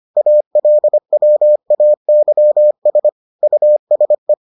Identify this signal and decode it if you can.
Morse code, heard as a keyed tone, about 25 wpm, 600 Hz. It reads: ALWAYS USE